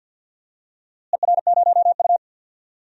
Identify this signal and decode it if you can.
Morse code, heard as a keyed tone, about 52 wpm, 700 Hz. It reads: ER0U
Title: DXCC-100 Call Signs - 4 Characters - Repeated 2x - 50wpm